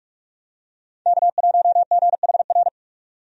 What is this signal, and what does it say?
Morse code, heard as a keyed tone, about 45 wpm, 700 Hz. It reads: K1GHR
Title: USA Call Signs - 5 Characters - 45wpm